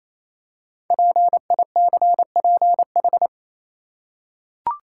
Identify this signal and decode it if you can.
Morse code, heard as a keyed tone, about 27 wpm, 700 Hz. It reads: PICPH
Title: Code Groups - 5 Letters - 28wpm